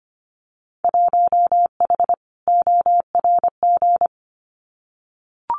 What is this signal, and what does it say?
Morse code, heard as a keyed tone, about 25 wpm, 700 Hz. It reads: 1H ORG
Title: Winter Field Day Exchanges 25wpm